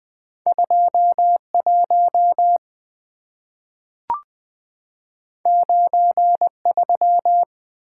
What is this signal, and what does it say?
Morse code, heard as a keyed tone, about 20 wpm, 700 Hz. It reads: 2 1 9 3
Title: Two Letter-Number Combinations - 20wpm